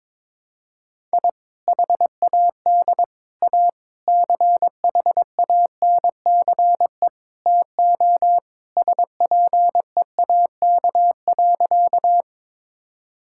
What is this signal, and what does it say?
Morse code, heard as a keyed tone, about 21 wpm, 700 Hz. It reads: I HAD A CHANCE TO SPEAK.
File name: practice / 22wpm / Sentences with 4-Word Phrases - 22wpm